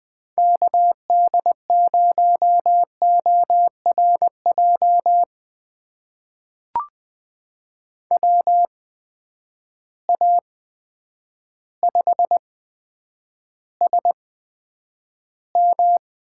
Morse code, heard as a keyed tone, about 20 wpm, spaced as for 9 wpm, 700 Hz. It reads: KD0ORJ WA5SM
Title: USA Call Signs - 6 Characters - 8x Character Spacing - 20wpm